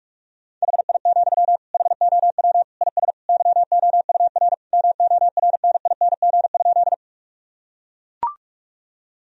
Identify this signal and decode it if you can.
Morse code, heard as a keyed tone, about 41 wpm, 700 Hz. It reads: HI, HOW IS YOUR MORNING?
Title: Sentences from Top 1000 Words - Review Part 3 - 45wpm